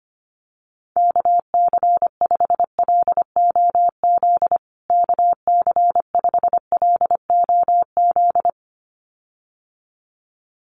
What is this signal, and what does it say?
Morse code, heard as a keyed tone, about 25 wpm, 700 Hz. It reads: KC5LOZ KC5LOZ
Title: USA Call Signs Repeated 2x 25wpm